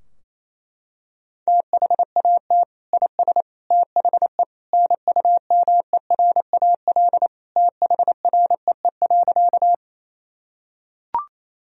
Morse code, heard as a keyed tone, about 28 wpm, 700 Hz. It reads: THAT IS THE NUMERAL THREE.
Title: Sentences from Top 1000 Words - Review Part 2 - 28wpm